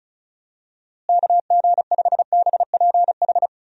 Morse code, heard as a keyed tone, about 34 wpm, 700 Hz. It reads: KG5BPH
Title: USA Call Signs - 6 Characters - Repeated 3x - 35wpm